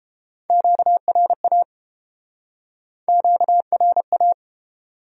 Morse code, heard as a keyed tone, about 31 wpm, 700 Hz. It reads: QRA QRA